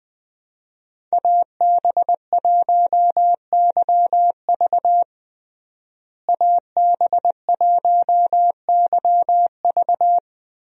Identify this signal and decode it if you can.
Morse code, heard as a keyed tone, about 20 wpm, 700 Hz. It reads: AB1YV AB1YV
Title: USA Call Signs - 5 Characters - Repeated 2x - 20wpm